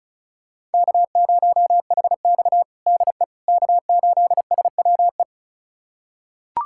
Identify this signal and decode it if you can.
Morse code, heard as a keyed tone, about 36 wpm, 700 Hz. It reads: K0HX DE K8SWE
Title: USA Callsign DE Callsign 35wpm